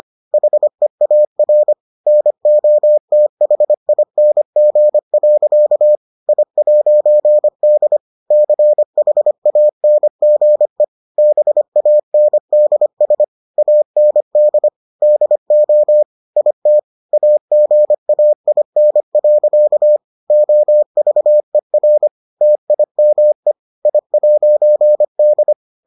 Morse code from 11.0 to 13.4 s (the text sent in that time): BANDS